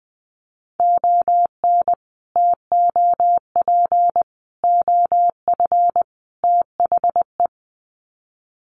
Morse code, heard as a keyed tone, about 20 wpm, 700 Hz. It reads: ON TOP OF THE
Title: Sets of 4 Words 20wpm